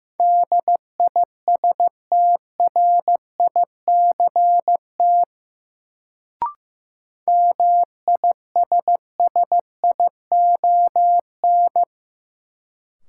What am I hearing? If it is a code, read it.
Morse code, heard as a keyed tone, about 15 wpm, 700 Hz. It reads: DISTRICT MISSION
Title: Sentences from Top 1600 Words - New Word Drill Alt - 15wpm